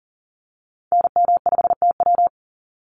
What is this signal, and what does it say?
Morse code, heard as a keyed tone, about 40 wpm, 700 Hz. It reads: NM5TW
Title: NA SOTA Activator Callsigns 40wpm